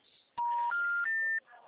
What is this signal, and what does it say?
Special information tone.